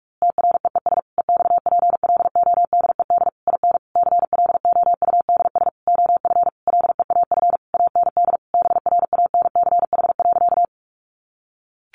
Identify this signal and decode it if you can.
Morse code, heard as a keyed tone, about 44 wpm, 700 Hz. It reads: TREES EXPLODED IN CLOUDS OF LEAF AND BRANCH.
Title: A Sound Of Thunder 45wpm